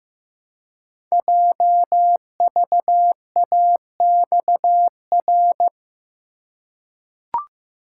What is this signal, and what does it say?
Morse code, heard as a keyed tone, about 15 wpm, 700 Hz. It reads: JVAXR
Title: Code Groups - 5 Letter:Numbers - 15wpm